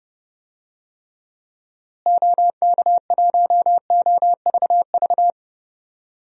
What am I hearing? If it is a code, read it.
Morse code, heard as a keyed tone, about 30 wpm, 700 Hz. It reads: OK1OVV
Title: DXCC-100 Call Signs - 6 Characters - Repeated 2x - 30wpm